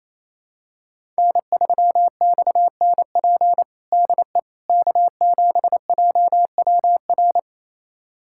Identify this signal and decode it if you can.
Morse code, heard as a keyed tone, about 27 wpm, 700 Hz. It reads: N3XNP DE K7JWR
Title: USA Callsign DE Callsign - 5 Characters - 28wpm